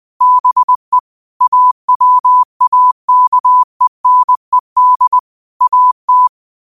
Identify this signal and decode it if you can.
Morse code, heard as a keyed tone, about 20 wpm, 1000 Hz. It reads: BE AWAKENED AT
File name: qotdfiles / 20wpm